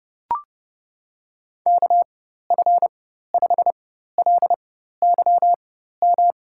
Morse code, heard as a keyed tone, about 30 wpm, spaced as for 14 wpm, 700 Hz. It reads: KF5LYM
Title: USA Call Signs - 6 Characters - 4x Character Spacing - 30wpm